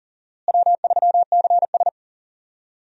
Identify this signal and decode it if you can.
Morse code, heard as a keyed tone, about 40 wpm, 700 Hz. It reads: W3CS
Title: USA Call Signs - 4 Characters - Repeated 2x - 40wpm